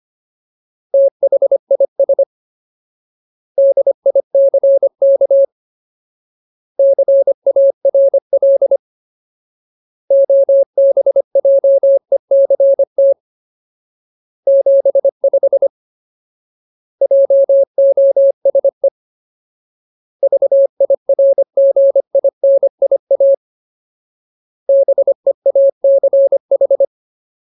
Morse code, heard as a keyed tone, about 25 wpm, 550 Hz. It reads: THIS DICK CARL OBJECT 75 JOSE VIRGINIA BEACH
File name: Ultimate Word List / 25WPM Ultimate